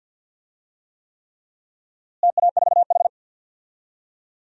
Morse code, heard as a keyed tone, about 48 wpm, 700 Hz. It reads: TA4H
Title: DXCC-100 Call Signs - 4 Characters - Repeated 3x - 50wpm